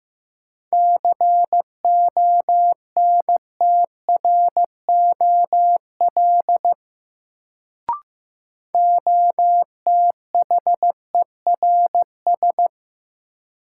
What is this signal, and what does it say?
Morse code, heard as a keyed tone, about 15 wpm, 700 Hz. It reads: CONTROL OTHERS
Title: Sentences from Top 1000 Words - Review Part 2 - New Word Drill - 15wpm